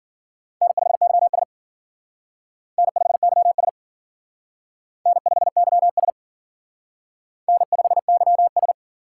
Morse code, heard as a keyed tone, about 50 wpm, 700 Hz. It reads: N5YS N5YS N5YS N5YS
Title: Speed Racing - USA Call Signs 40wpm